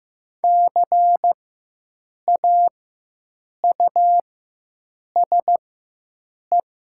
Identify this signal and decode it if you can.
Morse code, heard as a keyed tone, about 15 wpm, spaced as for 7 wpm, 700 Hz. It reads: CAUSE
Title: 4x Character Spacing - Top 200 Words - 15wpm